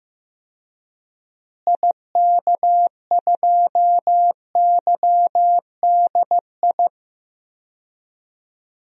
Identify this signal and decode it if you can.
Morse code, heard as a keyed tone, about 15 wpm, 700 Hz. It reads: IK2YDI
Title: DXCC-100 Call Signs - 6 Characters - Repeated 3x - 15wpm